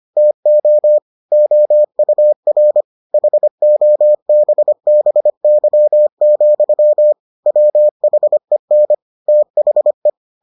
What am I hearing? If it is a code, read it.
Morse code, heard as a keyed tone, about 25 wpm, 600 Hz. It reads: TO OUR HOBBY, WHEN THE